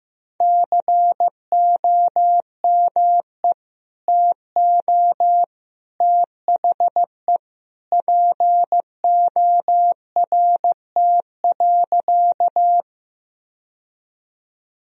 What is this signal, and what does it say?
Morse code, heard as a keyed tone, about 15 wpm, 700 Hz. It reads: COME TO THE PORT.
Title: Sentences from Top 1000 Words - Review Part 1 - 15wpm